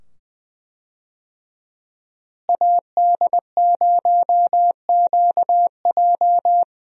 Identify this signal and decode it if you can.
Morse code, heard as a keyed tone, about 20 wpm, 700 Hz. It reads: AD0QJ
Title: USA Call Signs - 5 Characters - Repeated 2x - 20wpm